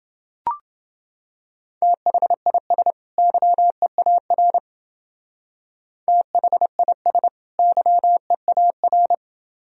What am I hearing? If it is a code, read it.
Morse code, heard as a keyed tone, about 29 wpm, 700 Hz. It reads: THIS YEAR THIS YEAR